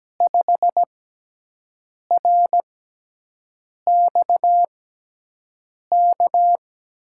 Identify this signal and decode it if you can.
Morse code, heard as a keyed tone, about 17 wpm, 700 Hz. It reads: 5 R X K
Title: USA Call Signs - 6 Characters - 6x Character Spacing - 17wpm